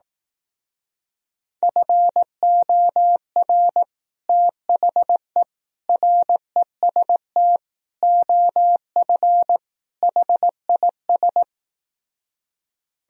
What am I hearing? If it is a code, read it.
Morse code, heard as a keyed tone, about 18 wpm, 700 Hz. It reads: FOR THE REST OF HIS
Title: Speed-Racing - Sets of 5 Words 15wpm